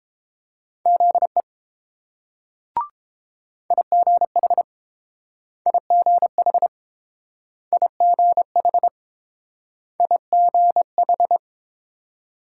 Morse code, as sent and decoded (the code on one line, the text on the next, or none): --.. / .. --. .... / .. --. .... / .. --. .... / .. --. ....
ZE IGH IGH IGH IGH